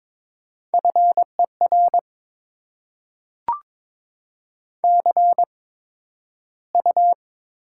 Morse code, heard as a keyed tone, about 22 wpm, spaced as for 12 wpm, 700 Hz. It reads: FER CU
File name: Top 20 QSO Elements - 8x Character Spacing - 22wpm